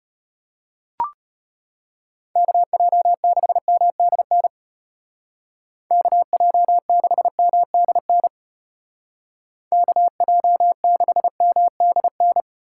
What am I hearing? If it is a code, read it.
Morse code, heard as a keyed tone, about 33 wpm, 700 Hz. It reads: KJ6MDN KJ6MDN KJ6MDN